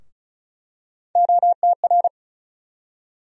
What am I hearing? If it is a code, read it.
Morse code, heard as a keyed tone, about 34 wpm, 700 Hz. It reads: OTR
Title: USA Call Sign Postfixes - 3 Letters 35wpm